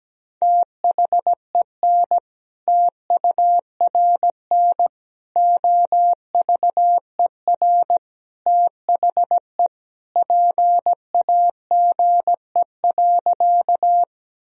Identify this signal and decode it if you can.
Morse code, heard as a keyed tone, about 17 wpm, 700 Hz. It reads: THEN TURN OVER THE PAGE.
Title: Sentences from Top 1000 Words - Review Part 1 - 17wpm